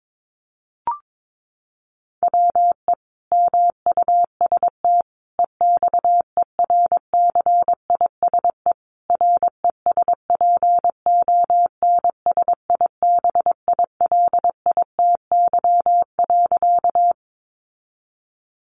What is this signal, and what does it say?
Morse code, heard as a keyed tone, about 22 wpm, 700 Hz. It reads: WE MUST EXERCISE RESPONSIBILITY.
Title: Sentences from Top 1200 Words 22wpm